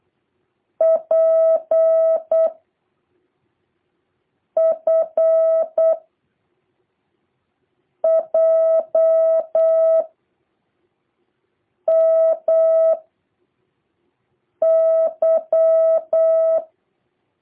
Morse code, heard as a keyed tone, about 8 wpm, 650 Hz. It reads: P F J M Y
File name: jh8cbh / 2023cw